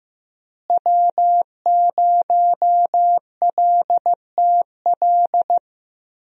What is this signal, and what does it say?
Morse code, heard as a keyed tone, about 15 wpm, 700 Hz. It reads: W0LTL
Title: USA Call Signs - 5 Characters - Repeated 3x - 15wpm